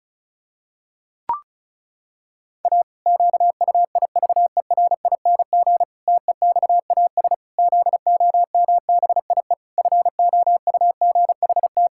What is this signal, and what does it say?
Morse code, heard as a keyed tone, about 36 wpm, 700 Hz. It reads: A QUIVERING TEXAS ZOMBIE FOUGHT
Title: Pangram Challenge v2 - 35wpm